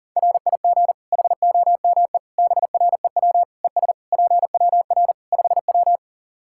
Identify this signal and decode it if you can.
Morse code, heard as a keyed tone, about 40 wpm, 700 Hz. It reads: RIG HOME BREW ES PWR 5W
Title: Practice QSO - Exchange 2 - Info - 40wpm